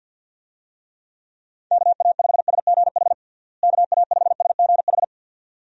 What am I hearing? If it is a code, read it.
Morse code, heard as a keyed tone, about 51 wpm, 700 Hz. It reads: KA5SGH KA5SGH
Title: USA Call Signs - 6 Characters - Repeated 2x - 50wpm